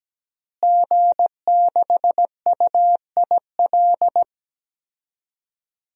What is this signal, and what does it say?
Morse code, heard as a keyed tone, about 17 wpm, 700 Hz. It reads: G6UIL
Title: DXCC-100 Call Signs - 5 Characters - Repeated 3x - 17wpm